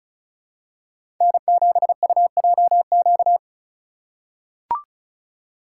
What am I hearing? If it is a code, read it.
Morse code, heard as a keyed tone, about 36 wpm, 700 Hz. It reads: N7UJQ
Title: USA Call Signs - 2x Character Spacing - 35wpm